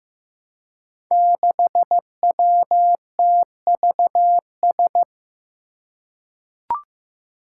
Morse code, heard as a keyed tone, about 15 wpm, 700 Hz. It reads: 6WTVS
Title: Five Challenging Letter-Number Combinations 15wpm